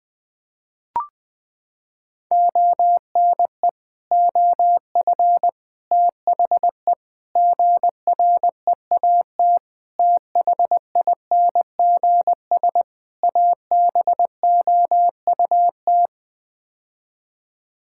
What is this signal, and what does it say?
Morse code, heard as a keyed tone, about 20 wpm, 700 Hz. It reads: ONE OF THE GREAT THINGS ABOUT
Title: Sets of 6 Words - 20wpm